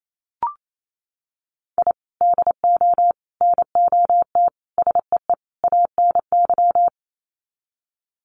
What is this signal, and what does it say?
Morse code, heard as a keyed tone, about 28 wpm, 700 Hz. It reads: I DO NOT SEE ANY
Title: Sets of 5 Words 28wpm